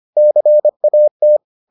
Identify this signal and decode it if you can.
Morse code, heard as a keyed tone, about 25 wpm, 600 Hz. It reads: CAT